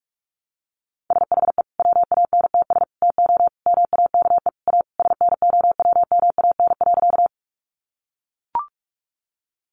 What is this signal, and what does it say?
Morse code, heard as a keyed tone, about 44 wpm, 700 Hz. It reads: SHE WANTS TO MAKE A SNOWMAN.
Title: Easy Phrases 45wpm